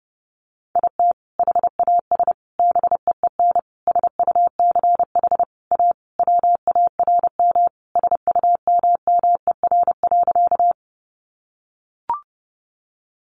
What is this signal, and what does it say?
Morse code, heard as a keyed tone, about 30 wpm, 700 Hz. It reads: IT HAS BEEN SUCH A WARM SUMMER.
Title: Sentences from Top 1100 Words 30wpm